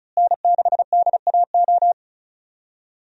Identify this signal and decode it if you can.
Morse code, heard as a keyed tone, about 36 wpm, 700 Hz. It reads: N6DAO
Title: USA Call Signs - 5 Characters - Repeated 2x - 35wpm